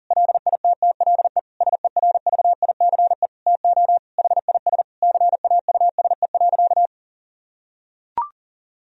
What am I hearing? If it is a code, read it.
Morse code, heard as a keyed tone, about 37 wpm, 700 Hz. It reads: LITTLE SERVICE TO HIS CAUSE.
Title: Sentences from Top 1000 Words - Review Part 4 - 40wpm